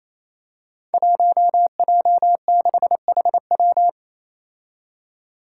Morse code, heard as a keyed tone, about 28 wpm, 700 Hz. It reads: 1J6HW
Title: Five Challenging Letter-Number Combinations 28wpm